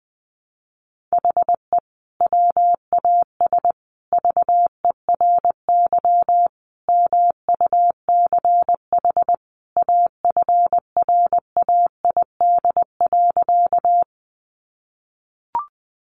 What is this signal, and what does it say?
Morse code, heard as a keyed tone, about 20 wpm, 700 Hz. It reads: HE WAS VERY MUCH AFRAID.
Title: Sentences from Top 1500 Words 20wpm